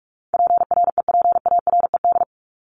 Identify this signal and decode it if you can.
Morse code, heard as a keyed tone, about 44 wpm, 700 Hz. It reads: PREPARED
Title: ED Suffix - Top 100 Words with Base Words 45wpm